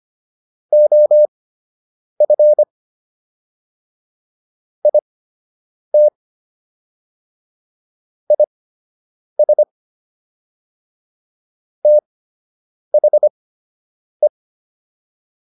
Morse code, heard as a keyed tone, about 25 wpm, spaced as for 8 wpm, 600 Hz. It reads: OF IT IS THE